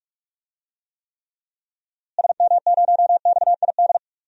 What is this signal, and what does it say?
Morse code, heard as a keyed tone, about 45 wpm, 700 Hz. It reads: SM0XID
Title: DXCC-100 Call Signs - 6 Characters - Repeated 2x - 45wpm